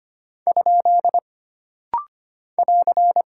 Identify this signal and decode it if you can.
Morse code, heard as a keyed tone, about 25 wpm, 700 Hz. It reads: ? <AR>
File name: Single Punctuation-Prosign - Rapid-Fire - 25wpm